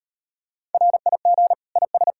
Morse code, heard as a keyed tone, about 38 wpm, 700 Hz. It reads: RIG IS